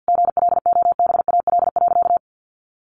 Morse code, heard as a keyed tone, about 50 wpm, 700 Hz. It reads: GLOBAL.
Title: Sentences from Top 1200 Words 50wpm